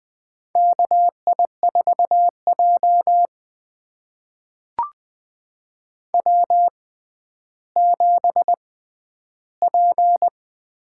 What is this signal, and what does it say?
Morse code, heard as a keyed tone, about 20 wpm, spaced as for 11 wpm, 700 Hz. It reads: KI4J W7P
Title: USA Call Signs - 4 Characters - 6x Character Spacing - 20wpm